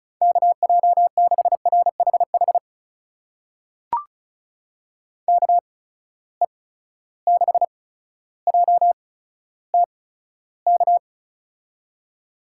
Morse code, heard as a keyed tone, about 36 wpm, spaced as for 15 wpm, 700 Hz. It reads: KJ6RHH KE6JTK
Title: USA Call Signs - 6 Characters - 8x Character Spacing - 35wpm